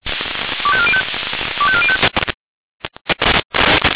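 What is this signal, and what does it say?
Five-tone selective call (ZVEI), heard as quick runs of five tones, tones of about 70 ms. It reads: 25405 25405
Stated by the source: ZVEI five-tone call received on SDR.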